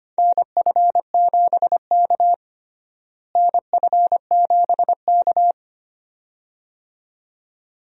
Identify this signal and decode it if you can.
Morse code, heard as a keyed tone, about 25 wpm, 700 Hz. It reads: NF7K NF7K
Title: USA Call Signs - 4 Characters - Repeated 2x - 25wpm